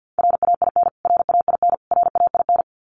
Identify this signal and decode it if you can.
Morse code, heard as a keyed tone, about 50 wpm, 700 Hz. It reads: RAIN RAIN RAIN
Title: Top 500 Words Repeated 50wpm